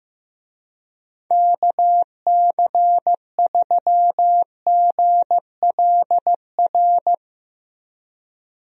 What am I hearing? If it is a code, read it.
Morse code, heard as a keyed tone, about 15 wpm, 700 Hz. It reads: KC3GLR
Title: USA Call Signs - 6 Characters - Repeated 3x - 15wpm